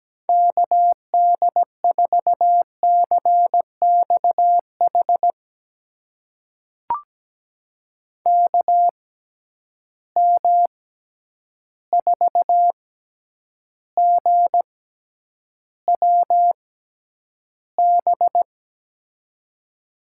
Morse code, heard as a keyed tone, about 17 wpm, spaced as for 9 wpm, 700 Hz. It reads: KD4CXH KM4GWB
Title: USA Call Signs - 6 Characters - 6x Character Spacing - 17wpm